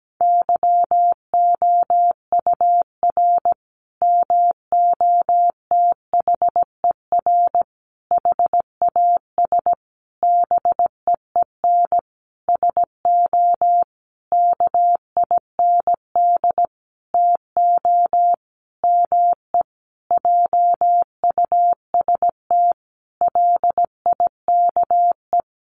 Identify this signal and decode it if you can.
Morse code, heard as a keyed tone, about 17 wpm, 700 Hz. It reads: YOUR MOTHER HAS BEEN SO KIND TO ME JUST LIKE
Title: Sentences from Top 1200 Words 17wpm